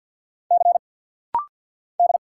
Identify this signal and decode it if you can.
Morse code, heard as a keyed tone, about 49 wpm, 700 Hz. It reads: C D
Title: Single Letter-Number Alt - Rapid-Fire 50wpm